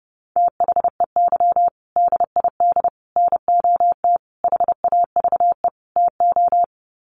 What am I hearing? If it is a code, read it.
Morse code, heard as a keyed tone, about 30 wpm, 700 Hz. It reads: THEY DID NOT HAVE TO
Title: Sets of 5 Words Encore2 30wpm